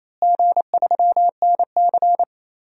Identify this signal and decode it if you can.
Morse code, heard as a keyed tone, about 28 wpm, 700 Hz. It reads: G3NC
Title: DXCC-100 Call Signs - 4 Characters - Repeated 3x - 28wpm